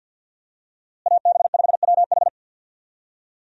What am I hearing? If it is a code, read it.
Morse code, heard as a keyed tone, about 50 wpm, 700 Hz. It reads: AB5WH
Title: USA Call Signs - 5 Characters - Repeated 2x - 50wpm